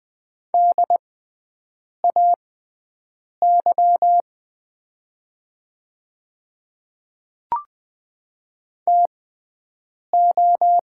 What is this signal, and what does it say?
Morse code, heard as a keyed tone, about 20 wpm, spaced as for 7 wpm, 700 Hz. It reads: DAY TO
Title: 6x Character Spacing - Top 100 Words - 20wpm